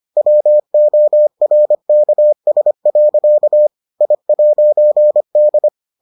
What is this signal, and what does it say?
Morse code, heard as a keyed tone, about 25 wpm, 600 Hz. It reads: WORKS. I'D